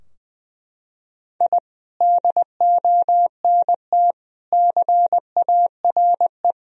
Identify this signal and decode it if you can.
Morse code, heard as a keyed tone, about 21 wpm, 700 Hz. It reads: I DONT CARE
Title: Sentences from Top 1000 Words - Review Part 4 - 20wpm